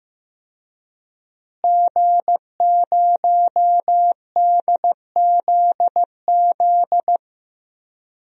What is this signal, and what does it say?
Morse code, heard as a keyed tone, about 15 wpm, 700 Hz. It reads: G0DZZ
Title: DXCC-100 Call Signs - 5 Characters - Repeated 3x - 15wpm